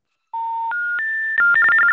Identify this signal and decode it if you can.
Special information tone, with noise.